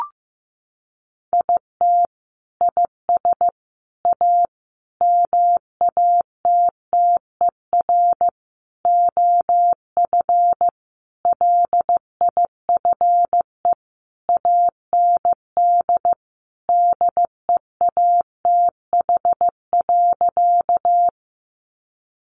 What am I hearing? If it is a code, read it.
Morse code, heard as a keyed tone, about 15 wpm, 700 Hz. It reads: IT IS A MATTER OF LIFE AND DEATH.
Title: Sentences from Top 1100 Words 15wpm